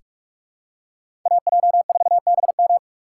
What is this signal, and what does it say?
Morse code, heard as a keyed tone, about 44 wpm, 690 Hz. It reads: AJ4BM
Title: USA Call Signs - 5 Characters - 6x Character Spacing - 45wpm